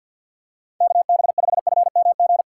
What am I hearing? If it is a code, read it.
Morse code, heard as a keyed tone, about 52 wpm, 700 Hz. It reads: KB5FMG
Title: USA Call Signs - 6 Characters - 4x Character Spacing - 50wpm